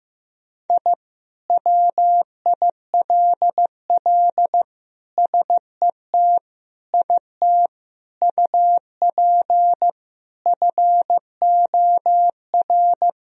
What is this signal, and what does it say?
Morse code, heard as a keyed tone, about 15 wpm, 700 Hz. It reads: I WILL SET IT UP FOR
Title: Sentences from Top 1000 Words - Review Part 1 - 15wpm